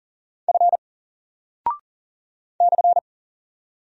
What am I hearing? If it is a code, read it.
Morse code, heard as a keyed tone, about 40 wpm, 700 Hz. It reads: F /
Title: Single Character (Letter:Number:Punctuation:Prosign) 40wpm